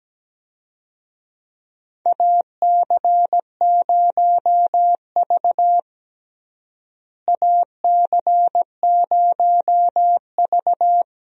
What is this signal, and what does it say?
Morse code, heard as a keyed tone, about 17 wpm, 700 Hz. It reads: AC0V AC0V
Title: USA Call Signs - 4 Characters - Repeated 3x - 17wpm